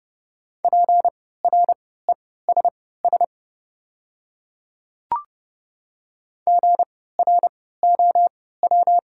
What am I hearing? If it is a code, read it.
Morse code, heard as a keyed tone, about 30 wpm, spaced as for 17 wpm, 700 Hz. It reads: PRESS GROW
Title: 3x Character Spacing - Top 300 Words - 30wpm